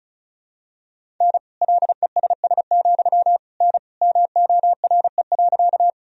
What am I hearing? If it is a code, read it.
Morse code, heard as a keyed tone, about 36 wpm, 700 Hz. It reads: N LESS, N MORE.
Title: Sentences from Top 400 Words and Abbreviations 35wpm